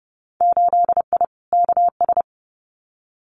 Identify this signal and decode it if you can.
Morse code, heard as a keyed tone, about 30 wpm, 700 Hz. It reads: 8I KS
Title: Winter Field Day Exchanges Encore 30wpm